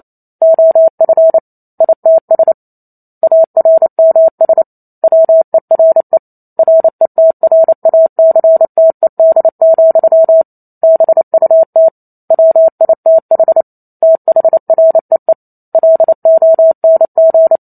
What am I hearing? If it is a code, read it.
Morse code, heard as a keyed tone, about 28 wpm, 650 Hz. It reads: OF ITS ARMS WERE RETRACTED, BUT WITH THREE LONG